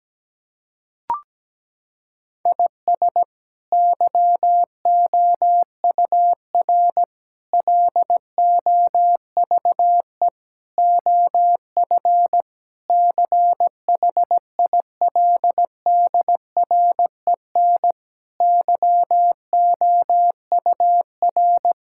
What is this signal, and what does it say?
Morse code, heard as a keyed tone, about 17 wpm, 700 Hz. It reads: IS YOUR LOVE OF CHILDREN YOUR
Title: Sentences from Top 1000 Words - Review Part 5 - 17wpm